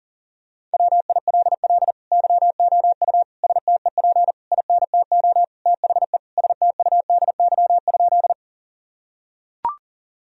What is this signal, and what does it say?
Morse code, heard as a keyed tone, about 40 wpm, 700 Hz. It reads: WILL YOU STEP INTO THE STUDY?
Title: Sentences from Top 1000 Words - Review Part 2 - 40wpm